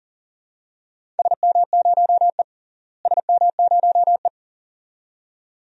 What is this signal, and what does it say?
Morse code, heard as a keyed tone, about 38 wpm, 700 Hz. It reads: SM0E SM0E
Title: DXCC-100 Call Signs - 4 Characters - Repeated 2x - 40wpm